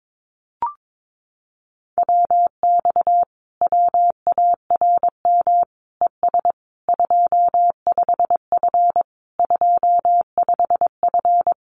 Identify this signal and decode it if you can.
Morse code, heard as a keyed tone, about 22 wpm, 700 Hz. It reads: WX WARM ES 25F 25F
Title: Common QSO Phrases Encore 22wpm